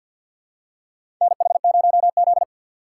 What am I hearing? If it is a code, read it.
Morse code, heard as a keyed tone, about 47 wpm, 700 Hz. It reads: NH0Z
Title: USA Call Signs - 3x Character Spacing - 50wpm